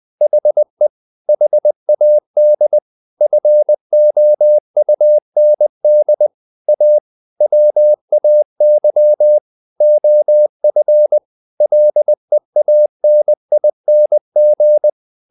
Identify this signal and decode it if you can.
Morse code, heard as a keyed tone, about 20 wpm, 600 Hz. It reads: HE HAD FOUND A WAY OF LEANING
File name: the old man and the sea / the old man and the sea part 1 20wpm